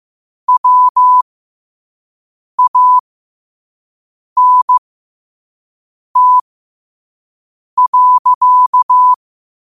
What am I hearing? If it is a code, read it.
Morse code, heard as a keyed tone, about 15 wpm, spaced as for 5 wpm, 1000 Hz. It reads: WANT.
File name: qotdfiles / 5wpm